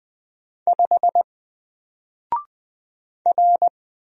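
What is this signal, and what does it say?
Morse code, heard as a keyed tone, about 20 wpm, 700 Hz. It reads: 5 R
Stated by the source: Single letter and number Morse drill at 20 wpm, from N4LCD.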